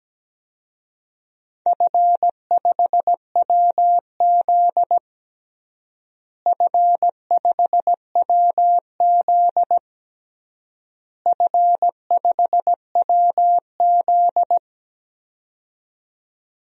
Morse code, heard as a keyed tone, about 17 wpm, 700 Hz. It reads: F5WZ F5WZ F5WZ
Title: DXCC-100 Call Signs - 4 Characters - Repeated 3x - 17wpm